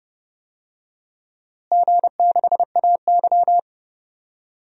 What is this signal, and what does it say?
Morse code, heard as a keyed tone, about 30 wpm, 700 Hz. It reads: G6AY
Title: DXCC-100 Call Signs - 4 Characters - Repeated 3x - 30wpm